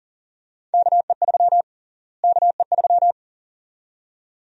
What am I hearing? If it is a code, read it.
Morse code, heard as a keyed tone, about 40 wpm, 700 Hz. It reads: KE3 KE3
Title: USA Call Sign Prefixes - Repeated 2x - 40wpm